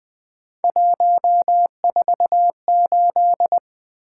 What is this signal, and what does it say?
Morse code, heard as a keyed tone, about 20 wpm, 700 Hz. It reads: 148
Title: Three Letter-Number Combinations - 20wpm